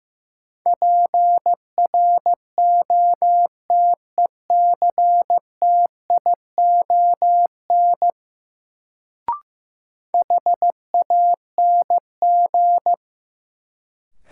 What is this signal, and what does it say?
Morse code, heard as a keyed tone, about 15 wpm, 700 Hz. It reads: PROTECTION HANG
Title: Sentences from Top 1600 Words - New Word Drill Alt - 15wpm